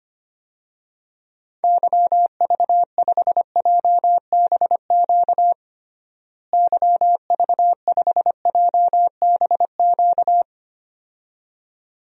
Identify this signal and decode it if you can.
Morse code, heard as a keyed tone, about 25 wpm, 700 Hz. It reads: YV5JBQ YV5JBQ
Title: DXCC-100 Call Signs - 6 Characters - Repeated 2x - 25wpm